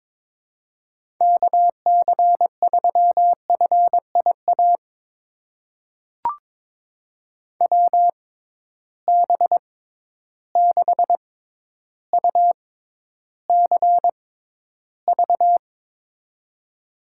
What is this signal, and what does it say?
Morse code, heard as a keyed tone, about 22 wpm, spaced as for 11 wpm, 700 Hz. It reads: KC3FIA WB6UCV
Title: USA Call Signs - 6 Characters - 6x Character Spacing - 22wpm